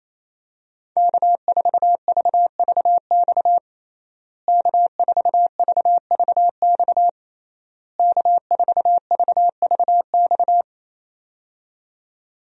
Morse code, heard as a keyed tone, about 27 wpm, 700 Hz. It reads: K4VVX K4VVX K4VVX
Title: USA Call Signs - 5 Characters - Repeated 3x - 28wpm